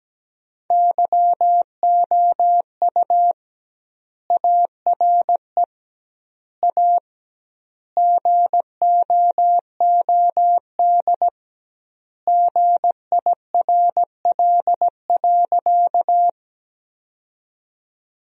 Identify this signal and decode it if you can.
Morse code, heard as a keyed tone, about 17 wpm, 700 Hz. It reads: YOU ARE A GOOD GIRL.
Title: 2x Word-Spacing - Sentences from Top 400 Words - 17wpm